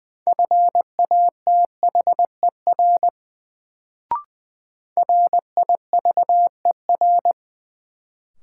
Morse code, heard as a keyed tone, about 20 wpm, 700 Hz. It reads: FATHER RIVER
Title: Sentences from Top 300 Words - New Word Drill - 20wpm